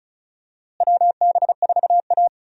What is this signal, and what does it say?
Morse code, heard as a keyed tone, about 36 wpm, 700 Hz. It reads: WB4A
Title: USA Call Signs - 4 Characters - Repeated 3x - 35wpm